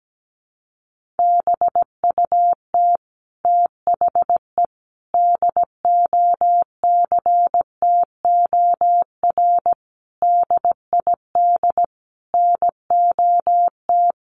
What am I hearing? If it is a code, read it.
Morse code, heard as a keyed tone, about 17 wpm, 700 Hz. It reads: BUT THE DOCTOR DID NOT
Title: Sentences from Top 1500 Words 17wpm